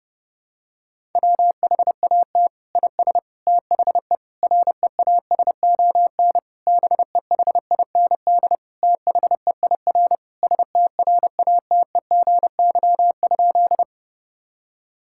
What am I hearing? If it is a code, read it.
Morse code, heard as a keyed tone, about 32 wpm, 700 Hz. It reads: WHAT IS THE REASON BEHIND THEIR STRATEGY?
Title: Sentences from Top 1000 Words - Review Part 5 - 30wpm